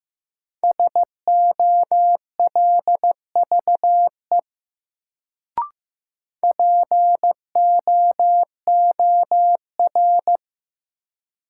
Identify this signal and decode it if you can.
Morse code, heard as a keyed tone, about 15 wpm, 700 Hz. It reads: SOLVE POOR
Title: Sentences from Top 1000 Words - New Word Drill - 15wpm